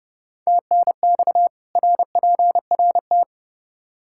Morse code, heard as a keyed tone, about 30 wpm, 700 Hz. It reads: TNX RPRT
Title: Practice QSO - Exchange 2 - Info - 30wpm